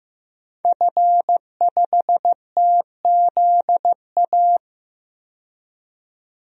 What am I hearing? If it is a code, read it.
Morse code, heard as a keyed tone, about 15 wpm, 700 Hz. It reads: F5TZA
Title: DXCC-100 Call Signs - 5 Characters - 15wpm